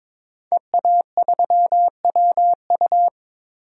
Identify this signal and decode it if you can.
Morse code, heard as a keyed tone, about 22 wpm, 700 Hz. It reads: EA3WU
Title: DXCC-100 Call Signs - 5 Characters - 22wpm